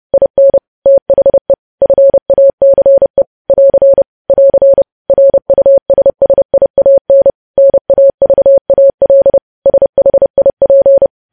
Morse code, heard as a keyed tone, about 30 wpm, 550 Hz. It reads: IN THE FACE <AR> <AR> RUSSIAN NAVAL SHIP
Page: News Headlines in Morse Code at 30 WPM